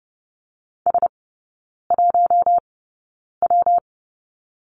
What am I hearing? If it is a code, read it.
Morse code, heard as a keyed tone, about 30 wpm, spaced as for 9 wpm, 700 Hz. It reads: S1W
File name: Lesson 19e - Letter w - Three Characters 30wpm